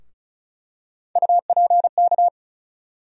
Morse code, heard as a keyed tone, about 36 wpm, 700 Hz. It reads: UPK
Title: USA Call Sign Postfixes - 3 Letters 35wpm